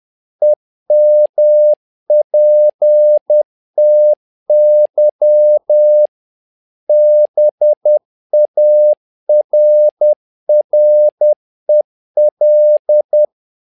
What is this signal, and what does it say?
Morse code, heard as a keyed tone, about 10 wpm, 600 Hz. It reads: EMPTY BARREL
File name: learning center / tower trouble / tower trouble 10wpm